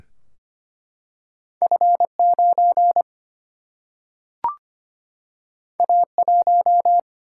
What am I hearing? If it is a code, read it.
Morse code, heard as a keyed tone, about 25 wpm, 700 Hz. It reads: F9 A1
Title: Single-Letter Single-Number 25wpm